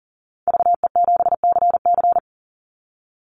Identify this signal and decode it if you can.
Morse code, heard as a keyed tone, about 40 wpm, 700 Hz. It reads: VE7CC
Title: Call Signs - 40wpm